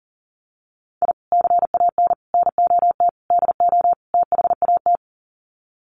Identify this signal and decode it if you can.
Morse code, heard as a keyed tone, about 40 wpm, 700 Hz. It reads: I CAN NOT DO THAT
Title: Sets of 5 Words Encore2 40wpm